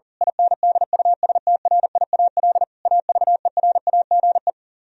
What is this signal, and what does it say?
Morse code, heard as a keyed tone, about 38 wpm, 700 Hz. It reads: INDUSTRIAL AVERAGE
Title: Speed-Racing - Sets of 5 Words 40wpm